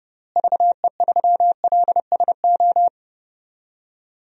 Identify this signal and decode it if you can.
Morse code, heard as a keyed tone, about 32 wpm, 700 Hz. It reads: VE3LSO
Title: DXCC-100 Call Signs - 6 Characters - Repeated 2x - 30wpm